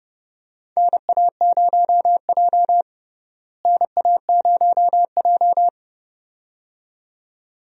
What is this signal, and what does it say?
Morse code, heard as a keyed tone, about 30 wpm, 700 Hz. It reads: NA0J NA0J
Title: USA Call Signs - 4 Characters - Repeated 2x - 30wpm